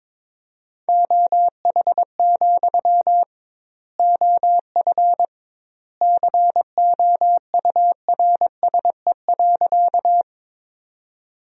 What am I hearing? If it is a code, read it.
Morse code, heard as a keyed tone, about 22 wpm, 700 Hz. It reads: OH, OF COURSE.
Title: 2x Word-Spacing - Sentences from Top 500 Words - 22wpm